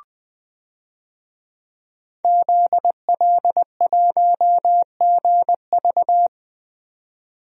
Morse code, heard as a keyed tone, about 20 wpm, 700 Hz. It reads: ZL1GV
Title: DXCC-100 Call Signs - 5 Characters - 20wpm